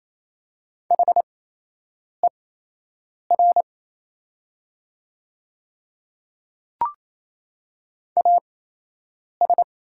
Morse code, heard as a keyed tone, about 28 wpm, spaced as for 8 wpm, 690 Hz. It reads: HER AS
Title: 8x Character Spacing - Top 100 Words - 28wpm